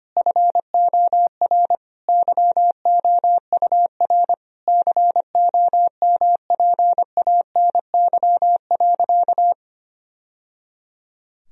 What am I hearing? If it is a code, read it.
Morse code, heard as a keyed tone, about 25 wpm, 700 Hz. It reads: FOR YOUR COMPANY.
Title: Sentences from Top 1000 Words - Review Part 3 - 25wpm